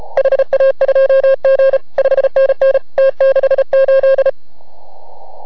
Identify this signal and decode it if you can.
Morse code, heard as a keyed tone, about 33 wpm, 540 Hz. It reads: HA2G 5NN T68